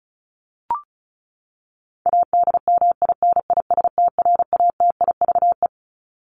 Morse code, heard as a keyed tone, about 34 wpm, 700 Hz. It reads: ADMINISTRATIVE
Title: Top 1501-1750 Words 35wpm